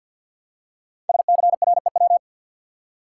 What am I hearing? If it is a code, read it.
Morse code, heard as a keyed tone, about 46 wpm, 700 Hz. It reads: SCREW
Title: Five Letter Words - 50wpm